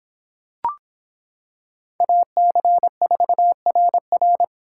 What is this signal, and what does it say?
Morse code, heard as a keyed tone, about 26 wpm, 700 Hz. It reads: AC4RR